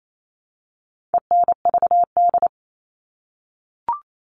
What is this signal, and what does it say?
Morse code, heard as a keyed tone, about 28 wpm, 700 Hz. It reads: ENVD
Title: All Cut Numbers - 4-Digit 28wpm